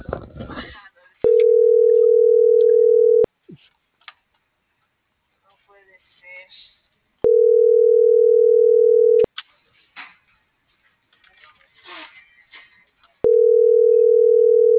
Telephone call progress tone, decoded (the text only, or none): ringback tone